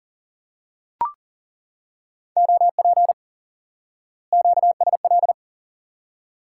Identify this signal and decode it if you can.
Morse code, heard as a keyed tone, about 40 wpm, 700 Hz. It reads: OP QSL
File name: IQE Course 3 - Lesson 89 - YAGI - Two Elements - 40wpm